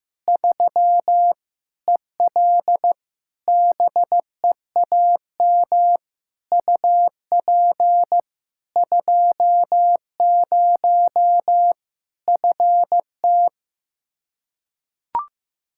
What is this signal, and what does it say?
Morse code, heard as a keyed tone, about 15 wpm, 700 Hz. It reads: 3 EL BEAM UP 20 FT
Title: Practice QSO - Exchange 2 - Info - 15wpm